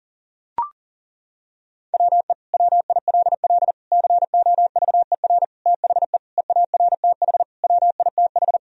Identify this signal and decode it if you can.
Morse code, heard as a keyed tone, about 40 wpm, 700 Hz. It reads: WE WILL COVER THE EARTH WITH
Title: Sentences from Top 1000 Words - Review Part 1 - 40wpm